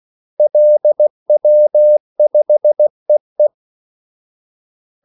Morse code, heard as a keyed tone, about 16 wpm, 600 Hz. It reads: LW5EE
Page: Kaldesignaler 16 wpm